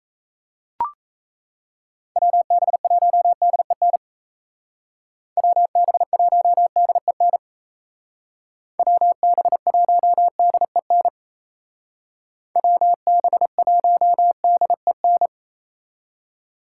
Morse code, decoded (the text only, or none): WB1DEN WB1DEN WB1DEN WB1DEN